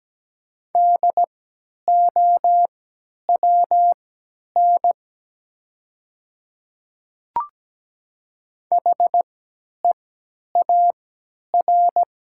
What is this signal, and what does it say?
Morse code, heard as a keyed tone, about 17 wpm, spaced as for 10 wpm, 700 Hz. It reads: DOWN HEAR
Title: 3x Character Spacing - Top 500 Words - 17wpm